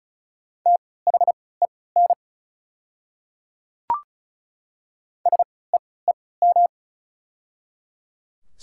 Morse code, heard as a keyed tone, about 36 wpm, spaced as for 20 wpm, 690 Hz. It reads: THEN SEEM
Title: 3x Character Spacing - Top 500 Words - 35wpm